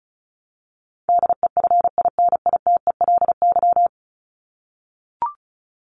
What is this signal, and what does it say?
Morse code, heard as a keyed tone, about 34 wpm, 700 Hz. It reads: DEFINITELY
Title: Top 1751-2000 Words 35wpm